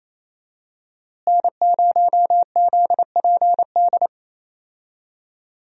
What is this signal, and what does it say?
Morse code, heard as a keyed tone, about 27 wpm, 700 Hz. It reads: N0ZPD
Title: USA Call Signs - 5 Characters - Repeated 3x - 28wpm